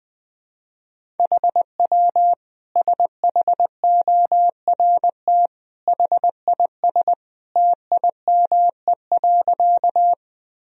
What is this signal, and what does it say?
Morse code, heard as a keyed tone, about 20 wpm, 700 Hz. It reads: HW SHORT HIS TIME.
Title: Sentences from Top 400 Words and Abbreviations 20wpm